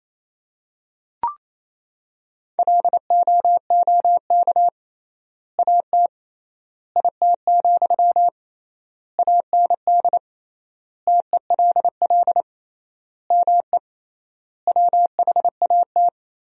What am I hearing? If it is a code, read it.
Morse code, heard as a keyed tone, about 28 wpm, 700 Hz. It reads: LOOK AT IT, AND TELL ME WHAT
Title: 3x Word-Spacing - Sentences from Top 200 Words - 28wpm